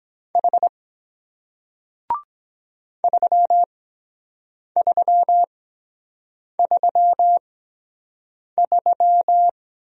Morse code, heard as a keyed tone, about 23 wpm, spaced as for 7 wpm, 700 Hz. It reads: H 3333